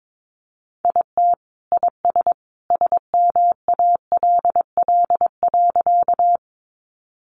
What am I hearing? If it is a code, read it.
Morse code, heard as a keyed tone, about 22 wpm, 700 Hz. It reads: IT IS SMALL.
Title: Additional Easy Phrases 22wpm